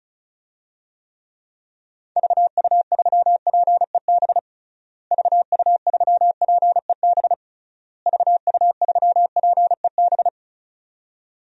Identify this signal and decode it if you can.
Morse code, heard as a keyed tone, about 36 wpm, 700 Hz. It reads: VU3PEB VU3PEB VU3PEB
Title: DXCC-100 Call Signs - 6 Characters - Repeated 3x - 35wpm